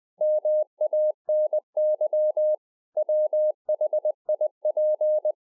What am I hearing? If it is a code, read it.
Morse code, heard as a keyed tone, about 20 wpm, 610 Hz. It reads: MANY WHIP